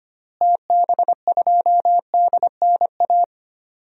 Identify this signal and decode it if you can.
Morse code, heard as a keyed tone, about 25 wpm, 700 Hz. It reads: TB2DNA
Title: DXCC-100 Call Signs - 6 Characters - Repeated 2x - 25wpm